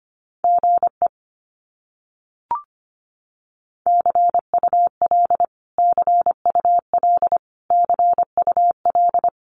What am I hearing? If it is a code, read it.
Morse code, heard as a keyed tone, about 25 wpm, 700 Hz. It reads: GE CUL CUL CUL
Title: Abbreviations Repeated 3x - 25wpm